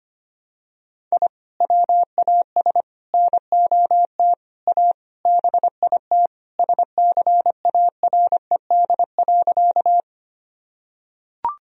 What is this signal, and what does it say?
Morse code, heard as a keyed tone, about 25 wpm, 700 Hz. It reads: I WAS NOT A BIT SCARED.
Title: Sentences from Top 1000 Words - Review Part 4 - 25wpm